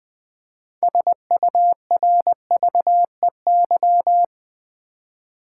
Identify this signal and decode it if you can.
Morse code, heard as a keyed tone, about 20 wpm, 700 Hz. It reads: SURVEY
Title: Six Letter Words - 20wpm